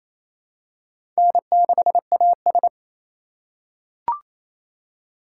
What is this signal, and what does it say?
Morse code, heard as a keyed tone, about 28 wpm, 700 Hz. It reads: N6AS
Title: USA Call Signs - 4 Characters - 3x Character Spacing - 28wpm